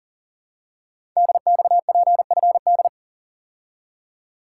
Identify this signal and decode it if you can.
Morse code, heard as a keyed tone, about 42 wpm, 690 Hz. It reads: DXPFD
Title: Code Groups - 5 Letters - 40wpm